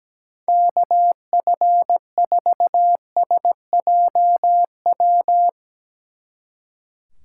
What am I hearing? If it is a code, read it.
Morse code, heard as a keyed tone, about 18 wpm, 700 Hz. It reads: KF4SJW
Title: Speed Racing - USA Call Signs 17wpm